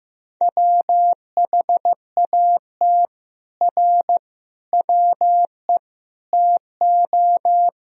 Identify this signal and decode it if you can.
Morse code, heard as a keyed tone, about 15 wpm, 700 Hz. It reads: WHAT R WE TO
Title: Sentences from Top 100 Words and Abbreviations 15wpm